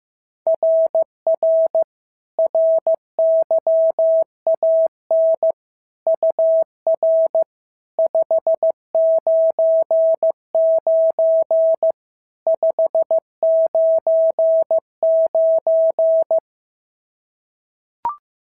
Morse code, heard as a keyed tone, about 15 wpm, 650 Hz. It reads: RR RYAN UR 599 599
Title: SOTA QSO Practice 15wpm